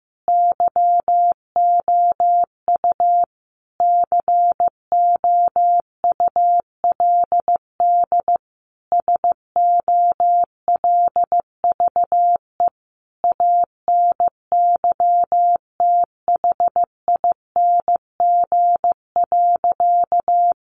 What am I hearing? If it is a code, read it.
Morse code, heard as a keyed tone, about 15 wpm, 700 Hz. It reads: YOU COULD SOLVE ANYTHING.
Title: Sentences from Top 1000 Words 15wpm